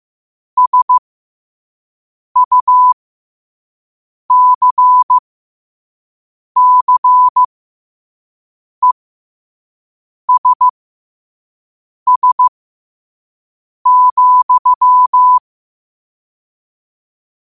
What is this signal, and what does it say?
Morse code, heard as a keyed tone, about 15 wpm, spaced as for 5 wpm, 1000 Hz. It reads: SUCCESS,
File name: qotdfiles / 5wpm